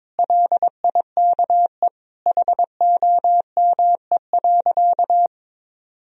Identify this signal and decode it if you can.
Morse code, heard as a keyed tone, about 22 wpm, 700 Hz. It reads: LIKE HOME.
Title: Sentences from Top 200 Words and Abbreviations 22wpm